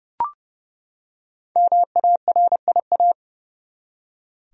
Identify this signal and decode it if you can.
Morse code, heard as a keyed tone, about 30 wpm, 700 Hz. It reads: MARIA
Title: USA Common Names - Max 5 Letters - 30wpm